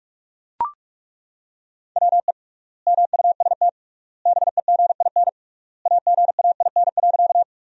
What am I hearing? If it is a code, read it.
Morse code, heard as a keyed tone, about 42 wpm, 700 Hz. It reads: WE MUST BEGIN AGAIN.
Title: 3x Word-Spacing - Sentences from Top 300 Words - 45wpm